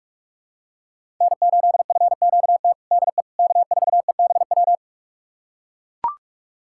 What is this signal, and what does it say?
Morse code, heard as a keyed tone, about 41 wpm, 700 Hz. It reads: N8FQT DE K4EBW
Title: USA Callsign DE Callsign - 5 Characters - 45wpm